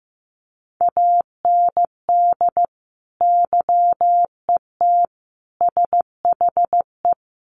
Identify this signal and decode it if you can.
Morse code, heard as a keyed tone, about 14 wpm, 700 Hz. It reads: AND YET SHE
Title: Sentences from Top 1100 Words 15wpm